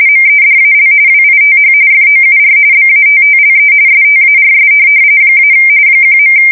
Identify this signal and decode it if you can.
RTTY, about 45.45 baud. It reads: MEDAL WITH FOUR BATTLE STARS - 1.
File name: fsk170 / USS-New-Jersey